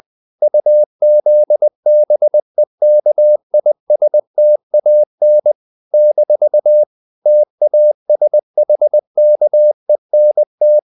Morse code, heard as a keyed tone, about 20 wpm, 600 Hz. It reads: UZBEKISTAN - TASHKENT
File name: CW / Practice / NationalCapitals20